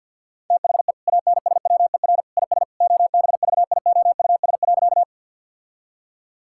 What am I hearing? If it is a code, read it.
Morse code, heard as a keyed tone, about 50 wpm, 700 Hz. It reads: THE ANSWER IS OBVIOUS.
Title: Sentences from Top 1600 Words 50wpm